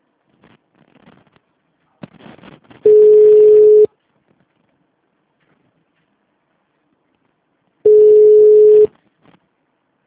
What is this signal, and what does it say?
Ringback tone (Europe, 425 Hz).